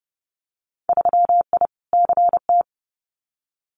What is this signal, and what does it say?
Morse code, heard as a keyed tone, about 30 wpm, 700 Hz. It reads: 3I CT
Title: Winter Field Day Exchanges 30wpm